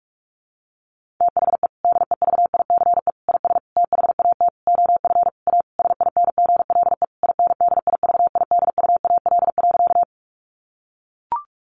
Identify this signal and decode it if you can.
Morse code, heard as a keyed tone, about 44 wpm, 700 Hz. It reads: THE DEVICE IS THAT OF A SINGLE INDIVIDUAL.
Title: Sentences from Top 1500 Words 45wpm